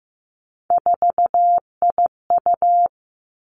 Morse code, heard as a keyed tone, about 15 wpm, 700 Hz. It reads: 4IU
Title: Three Challenging Letter-Number Combinations 15wpm